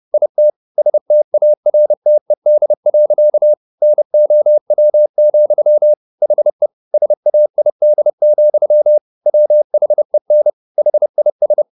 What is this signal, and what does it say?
Morse code, heard as a keyed tone, about 30 wpm, 600 Hz. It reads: IT STARTED. NOW, HE SAID, WHEN HIS
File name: the old man and the sea / the old man and the sea 1 30wpm